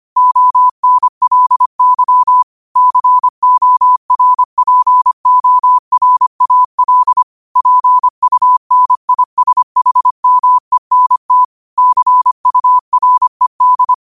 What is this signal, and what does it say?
Morse code, heard as a keyed tone, about 25 wpm, 1000 Hz. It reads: ONLY CORPORAL PUNISHMENT CURED